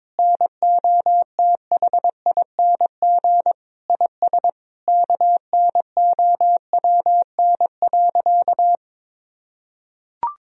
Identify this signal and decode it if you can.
Morse code, heard as a keyed tone, about 22 wpm, 700 Hz. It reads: NOTHING IS KNOWN.
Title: Sentences from Top 1000 Words - Review Part 4 - 22wpm